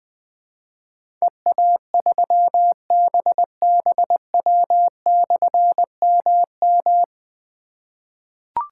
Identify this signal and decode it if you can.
Morse code, heard as a keyed tone, about 20 wpm, 700 Hz. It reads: EA3BBW/MM
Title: Difficult Call Signs 20wpm